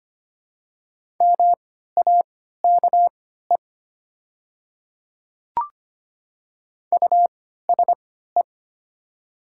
Morse code, heard as a keyed tone, about 25 wpm, spaced as for 14 wpm, 700 Hz. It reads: MAKE USE
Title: 3x Character Spacing - Top 200 Words - 25wpm